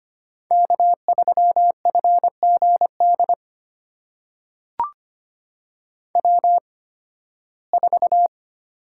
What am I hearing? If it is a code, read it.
Morse code, heard as a keyed tone, about 25 wpm, spaced as for 16 wpm, 700 Hz. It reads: K3FGD W4
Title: USA Call Signs - 5 Characters - 8x Character Spacing - 25wpm